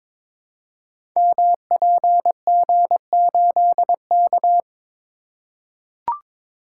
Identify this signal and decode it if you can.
Morse code, heard as a keyed tone, about 22 wpm, 690 Hz. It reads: MPG8K
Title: Code Groups - 5 Letter:Numbers - 22wpm